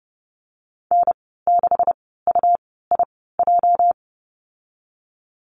Morse code, heard as a keyed tone, about 30 wpm, spaced as for 17 wpm, 700 Hz. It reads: N6UIJ
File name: Lesson 37g - Letter x - Call Signs 30wpm